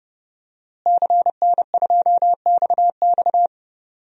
Morse code, heard as a keyed tone, about 30 wpm, 700 Hz. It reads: CN2XX
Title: DXCC-100 Call Signs - 5 Characters - Repeated 3x - 30wpm